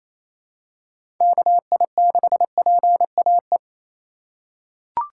Morse code, heard as a keyed tone, about 28 wpm, 700 Hz. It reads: KI6PAE
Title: USA Call Signs - 6 Characters - 8x Character Spacing - 28wpm